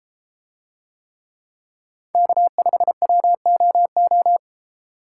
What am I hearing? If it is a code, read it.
Morse code, heard as a keyed tone, about 33 wpm, 700 Hz. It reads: K5WOO